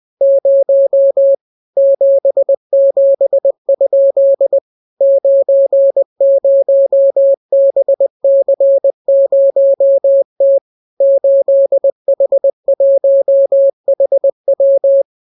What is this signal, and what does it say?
Morse code, heard as a keyed tone, about 20 wpm, 550 Hz. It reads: 0 77? 90BC0T 8H1HW